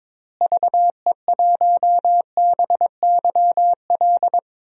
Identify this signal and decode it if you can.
Morse code, heard as a keyed tone, about 22 wpm, 700 Hz. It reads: VE1BYL
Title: DXCC-100 Call Signs - 6 Characters - Repeated 3x - 22wpm